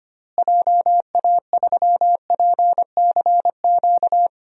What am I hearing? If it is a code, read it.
Morse code, heard as a keyed tone, about 25 wpm, 700 Hz. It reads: JA3PCQ
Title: DXCC-100 Call Signs - 6 Characters - Repeated 3x - 25wpm